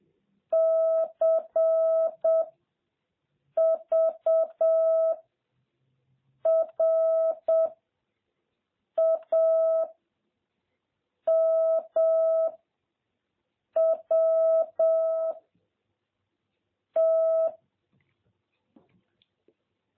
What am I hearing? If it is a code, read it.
Morse code, heard as a keyed tone, about 7 wpm, 650 Hz. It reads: C V R A M W T
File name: jh8cbh / 2021cw